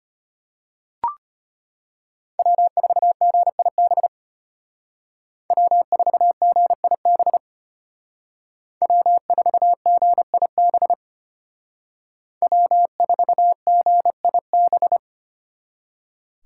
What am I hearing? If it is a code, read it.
Morse code, heard as a keyed tone, about 33 wpm, 700 Hz. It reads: W4GIB W4GIB W4GIB W4GIB